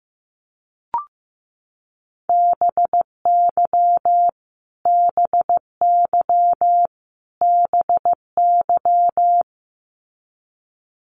Morse code, heard as a keyed tone, about 15 wpm, 700 Hz. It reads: BY BY BY
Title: Top 200 Words Repeated 15wpm